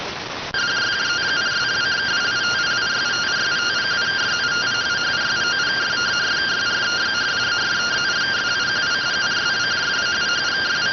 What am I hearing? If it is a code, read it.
Two-tone teleprinter signal, about 170 Hz shift.